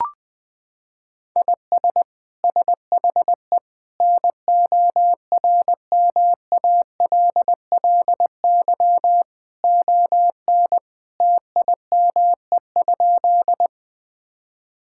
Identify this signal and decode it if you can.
Morse code, heard as a keyed tone, about 20 wpm, 700 Hz. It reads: IS SHE NORMALLY ON TIME?
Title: Sentences from Top 1000 Words - Review Part 6 - 20wpm